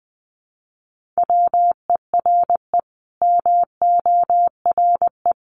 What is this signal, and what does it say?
Morse code, heard as a keyed tone, about 20 wpm, 700 Hz. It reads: WERE MORE
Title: Sets of 5 Words Encore2 20wpm